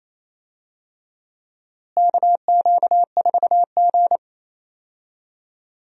Morse code, heard as a keyed tone, about 28 wpm, 700 Hz. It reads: KQ4G
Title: USA Call Signs - 4 Characters - Repeated 2x - 28wpm